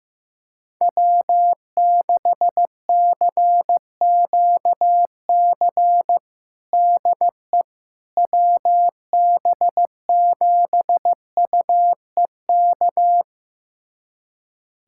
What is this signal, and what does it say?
Morse code, heard as a keyed tone, about 16 wpm, 700 Hz. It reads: W6CQC DE WB7UEK
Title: USA Callsign DE Callsign 15wpm